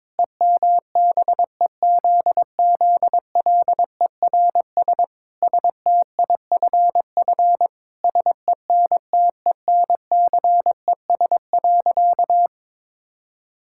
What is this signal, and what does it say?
Morse code, heard as a keyed tone, about 22 wpm, 700 Hz. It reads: EMBEZZLERS STIFF SENTENCES.
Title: Pangram Challenge v2 - 22wpm